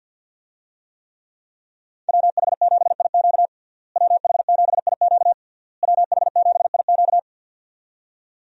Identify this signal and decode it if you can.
Morse code, heard as a keyed tone, about 53 wpm, 700 Hz. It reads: WH7IQ WH7IQ WH7IQ
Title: USA Call Signs - 5 Characters - Repeated 3x - 50wpm